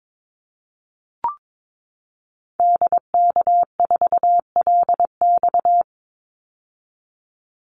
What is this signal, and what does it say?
Morse code, heard as a keyed tone, about 22 wpm, 700 Hz. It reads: DK4LX
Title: Call Signs - 22wpm